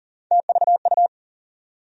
Morse code, heard as a keyed tone, about 40 wpm, 700 Hz. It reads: TVU
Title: USA Call Sign Postfixes - 3 Letters 40wpm